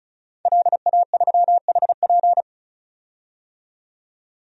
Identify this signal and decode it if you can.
Morse code, heard as a keyed tone, about 34 wpm, 700 Hz. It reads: LA3HP
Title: DXCC-100 Call Signs - 5 Characters - 35wpm